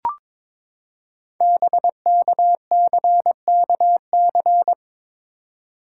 Morse code, heard as a keyed tone, about 22 wpm, 700 Hz. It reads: BKCKC
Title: Code Groups - 5 Letters - 22wpm